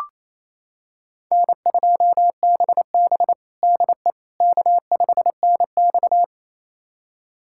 Morse code, heard as a keyed tone, about 28 wpm, 700 Hz. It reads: N2BB DE K5NX
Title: USA Callsign DE Callsign - 4 Characters - 28wpm